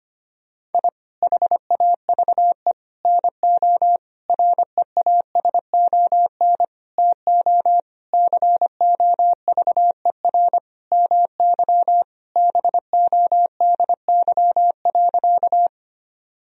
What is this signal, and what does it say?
Morse code, heard as a keyed tone, about 25 wpm, 700 Hz. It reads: I HAVE NO REASON TO COVER MY BODY.
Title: Sentences from Top 1000 Words - Review Part 1 - 25wpm